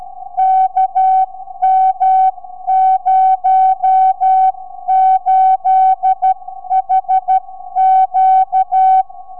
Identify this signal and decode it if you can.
Morse code, heard as a keyed tone, about 12 wpm, 760 Hz. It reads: KM08HQ
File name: sv8four 4m